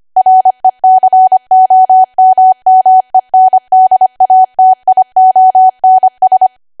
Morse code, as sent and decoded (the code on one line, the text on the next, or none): .-. -.-. --- -- -- . -. -.. .- - .. --- -. ...
RECOMMENDATIONS